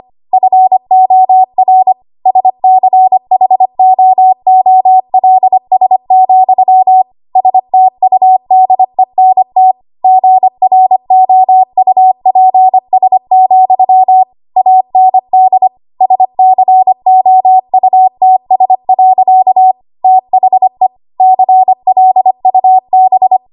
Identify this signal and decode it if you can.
Morse code, heard as a keyed tone, about 25 wpm, 750 Hz. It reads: FOR SCHOOLS, STUDENT GROUPS, AND SCOUTS. THE CLUB
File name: Morse / 25WPM